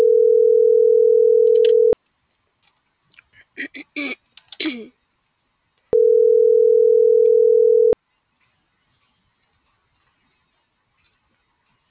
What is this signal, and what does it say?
Ringback tone (North America), 440 + 480 Hz.